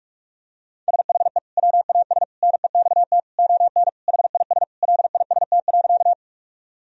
Morse code, heard as a keyed tone, about 41 wpm, 700 Hz. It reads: SHE WAS NEXT ON HIS LIST.
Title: Sentences from Top 1000 Words - Review Part 2 - 45wpm